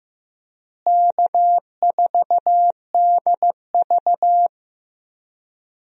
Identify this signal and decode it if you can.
Morse code, heard as a keyed tone, about 15 wpm, 700 Hz. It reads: K4DV
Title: USA Call Signs - 4 Characters - Repeated 2x - 15wpm